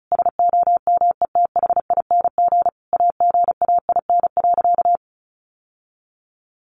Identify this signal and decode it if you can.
Morse code, heard as a keyed tone, about 34 wpm, 700 Hz. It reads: SOMETHING AGAIN.
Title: Sentences from Top 1100 Words 35wpm